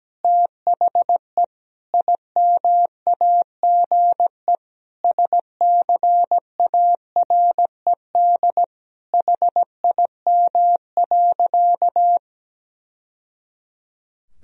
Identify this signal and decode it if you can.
Morse code, heard as a keyed tone, about 17 wpm, 700 Hz. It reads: THE IMAGE SCARED HIM.
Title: Sentences from Top 1000 Words - Review Part 4 - 17wpm